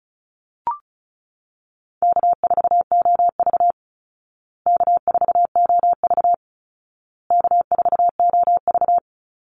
Morse code, heard as a keyed tone, about 34 wpm, 700 Hz. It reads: K4OV K4OV K4OV
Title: Call Signs Repeated 2x - 35wpm